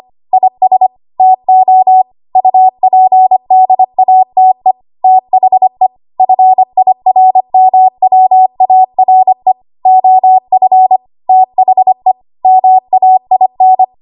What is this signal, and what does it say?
Morse code, heard as a keyed tone, about 25 wpm, 750 Hz. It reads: IS TO UPDATE THE FIRMWARE OF THE MAIN
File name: Archive / 25 WPM / 25WPM